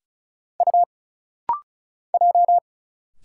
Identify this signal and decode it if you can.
Morse code, heard as a keyed tone, about 36 wpm, 700 Hz. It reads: U J
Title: Single Letter-Number - Rapid-Fire 35wpm